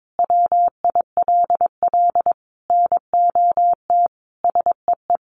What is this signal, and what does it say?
Morse code, heard as a keyed tone, about 22 wpm, 700 Hz. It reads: WILL NOT SEE
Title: Sentences from Top 600 Words 22wpm